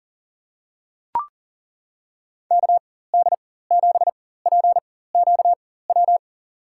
Morse code, heard as a keyed tone, about 40 wpm, spaced as for 19 wpm, 700 Hz. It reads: KD7PQW
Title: USA Call Signs - 6 Characters - 4x Character Spacing - 40wpm